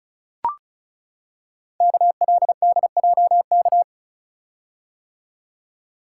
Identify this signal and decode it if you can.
Morse code, heard as a keyed tone, about 36 wpm, 700 Hz. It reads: KLDJK
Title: Code Groups - 5 Letters - 35wpm